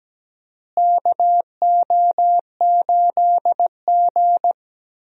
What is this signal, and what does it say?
Morse code, heard as a keyed tone, about 17 wpm, 700 Hz. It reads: KO8G
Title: USA Call Signs - 4 Characters - Repeated 2x - 17wpm